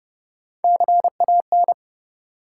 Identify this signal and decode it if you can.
Morse code, heard as a keyed tone, about 30 wpm, 700 Hz. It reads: CAN
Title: Top 180 QSO Elements 30wpm